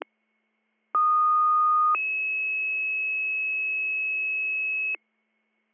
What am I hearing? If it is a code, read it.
Two-tone page, heard as a one-second tone, then a three-second tone, about 1230 then 2600 Hz, with noise. It is heard from 0.9 to 5.0 s.